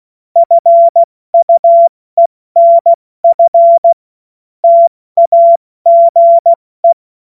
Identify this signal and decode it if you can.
Morse code, heard as a keyed tone, about 16 wpm, 680 Hz. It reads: FUENF TAGE
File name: Morsekurs / Witze / Witz 02 April 2024 70 F80